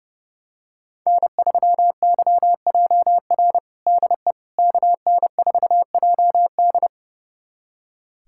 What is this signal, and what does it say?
Morse code, heard as a keyed tone, about 30 wpm, 700 Hz. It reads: N3YJR DE KN4JD
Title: USA Callsign DE Callsign - 5 Characters - 30wpm